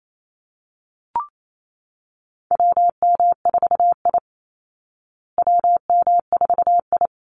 Morse code, heard as a keyed tone, about 28 wpm, 700 Hz. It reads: WM4I WM4I
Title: Call Signs Repeated 3x - 28wpm